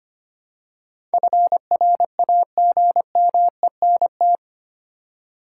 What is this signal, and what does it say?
Morse code, heard as a keyed tone, about 25 wpm, 700 Hz. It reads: FRAGMENT
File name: Eight Letter Words - 25wpm